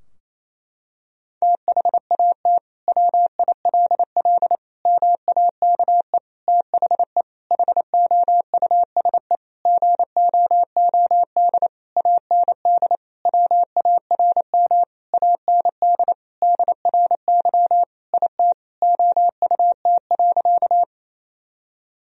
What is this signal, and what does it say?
Morse code, heard as a keyed tone, about 28 wpm, 700 Hz. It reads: THAT WILL MAKE THE HOUSE GOOD AND WARM AND DRY IT OUT.
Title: Sentences from Top 1000 Words - Review Part 3 - 28wpm